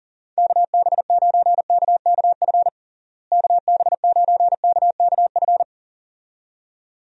Morse code, heard as a keyed tone, about 37 wpm, 700 Hz. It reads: KB9KKF KB9KKF
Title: USA Call Signs - 6 Characters - Repeated 2x - 40wpm